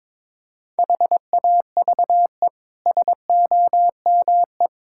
Morse code, heard as a keyed tone, about 22 wpm, 700 Hz. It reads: HAVE SOME
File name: practice / 22wpm / Easy Sentences with Hints - 22wpm